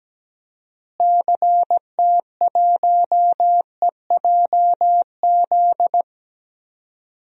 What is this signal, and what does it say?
Morse code, heard as a keyed tone, about 17 wpm, 700 Hz. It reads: CT1EJZ
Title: DXCC-100 Call Signs - 6 Characters - Repeated 2x - 17wpm